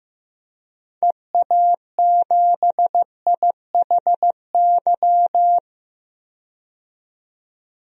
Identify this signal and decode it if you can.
Morse code, heard as a keyed tone, about 15 wpm, 700 Hz. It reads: EA7IHY
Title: DXCC-100 Call Signs - 6 Characters - 15wpm